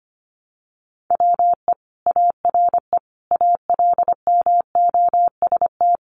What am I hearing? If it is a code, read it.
Morse code, heard as a keyed tone, about 25 wpm, 700 Hz. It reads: WE ARE ALMOST
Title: Sentences from Top 700 Words 25wpm